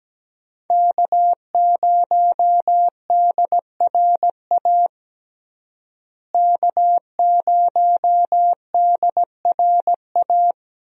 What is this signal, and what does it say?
Morse code, heard as a keyed tone, about 17 wpm, 700 Hz. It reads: K0DRA K0DRA
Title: USA Call Signs - 5 Characters - Repeated 2x - 17wpm